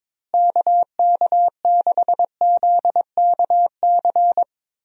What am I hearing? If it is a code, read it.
Morse code, heard as a keyed tone, about 22 wpm, 700 Hz. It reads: KK6ZKC
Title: USA Call Signs - 6 Characters - Repeated 3x - 22wpm